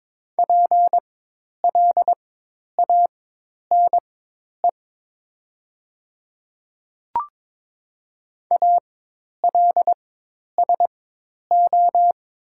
Morse code, heard as a keyed tone, about 22 wpm, spaced as for 10 wpm, 700 Hz. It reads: PLANE ALSO
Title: 4x Character Spacing - Top 500 Words - 22wpm